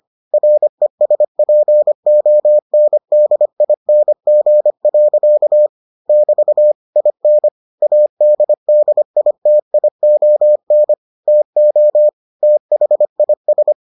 Morse code, heard as a keyed tone, about 25 wpm, 600 Hz. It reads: RESPONDING. <BT> IN ADDITION TO THIS